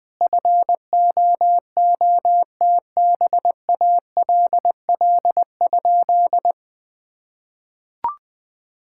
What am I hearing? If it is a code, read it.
Morse code, heard as a keyed tone, about 20 wpm, 700 Hz. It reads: FOOTBALL?
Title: Sentences from Top 1600 Words 20wpm